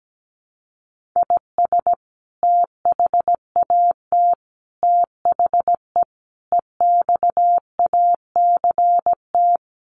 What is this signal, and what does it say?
Morse code, heard as a keyed tone, about 17 wpm, 700 Hz. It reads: IS THAT THE EXACT
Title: Sentences from Top 1500 Words 17wpm